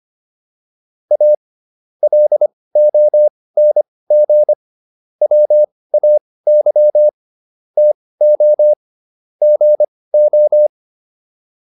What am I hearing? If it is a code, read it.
Morse code, heard as a keyed tone, about 25 wpm, spaced as for 18 wpm, 600 Hz. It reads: A LONG WAY TO GO